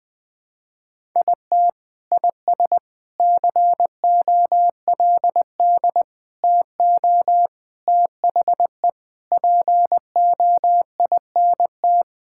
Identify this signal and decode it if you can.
Morse code, heard as a keyed tone, about 21 wpm, 700 Hz. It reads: IT IS COLD TO THE POINT
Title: Sentences with 4-Word Phrases - Spoken Beginning - 20wpm